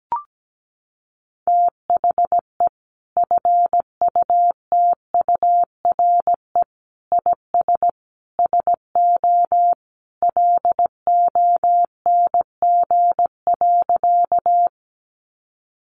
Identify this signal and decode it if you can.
Morse code, heard as a keyed tone, about 17 wpm, 700 Hz. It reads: THE FUTURE IS SO LONG.
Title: Sentences from Top 700 Words 17wpm